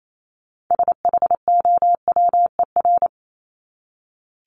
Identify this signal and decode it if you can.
Morse code, heard as a keyed tone, about 28 wpm, 700 Hz. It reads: SHOWER
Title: Top 1501-1750 Words 28wpm